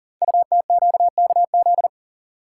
Morse code, heard as a keyed tone, about 42 wpm, 690 Hz. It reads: UTQKZ
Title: Code Groups - 5 Letters - 40wpm